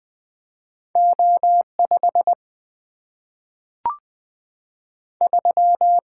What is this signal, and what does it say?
Morse code, heard as a keyed tone, about 20 wpm, 700 Hz. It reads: O 5 3
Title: Single-Letter Single-Number 20wpm